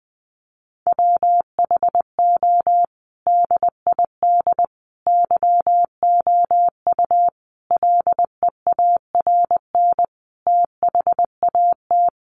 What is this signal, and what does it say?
Morse code, heard as a keyed tone, about 20 wpm, 700 Hz. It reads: WHO DID YOU LEARN THAT
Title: Sentences from Top 300 Words 20wpm